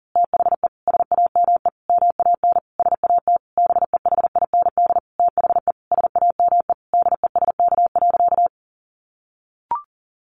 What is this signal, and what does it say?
Morse code, heard as a keyed tone, about 37 wpm, 700 Hz. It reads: THE SAME MAN SAT BEHIND THE SAME DESK.
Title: A Sound Of Thunder 40wpm